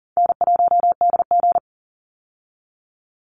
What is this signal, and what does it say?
Morse code, heard as a keyed tone, about 40 wpm, 700 Hz. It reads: N1DG
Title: Easy Call Signs Repeated 2x - 40wpm